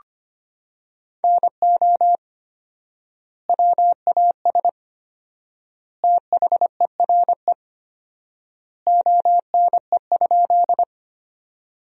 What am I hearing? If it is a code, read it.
Morse code, heard as a keyed tone, about 25 wpm, 700 Hz. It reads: NO WAS THERE ONE?
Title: 4x Word-Spacing - Sentences from Top 100 Words - 25wpm